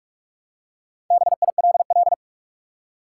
Morse code, heard as a keyed tone, about 46 wpm, 700 Hz. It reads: BILL
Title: Four Letter Words - 45wpm